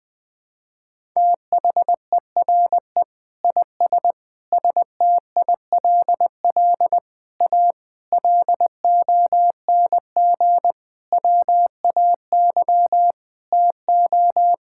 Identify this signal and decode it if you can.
Morse code, heard as a keyed tone, about 21 wpm, 700 Hz. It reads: THERE IS STILL A LONG WAY TO
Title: Sets of 8 Words - 20wpm